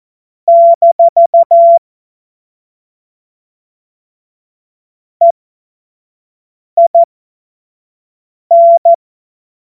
Morse code, heard as a keyed tone, about 14 wpm, spaced as for 5 wpm, 680 Hz. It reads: - EIN
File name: Morsekurs / Witze / Witz 20. Januar 2026 25 F70